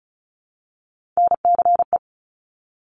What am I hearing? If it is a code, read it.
Morse code, heard as a keyed tone, about 36 wpm, 700 Hz. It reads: NCE
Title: Top Three Letter Combinations in Words 35wpm